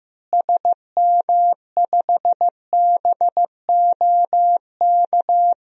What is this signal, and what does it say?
Morse code, heard as a keyed tone, about 16 wpm, 700 Hz. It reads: SM5BOK
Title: DXCC-100 Call Signs - 6 Characters - Repeated 3x - 15wpm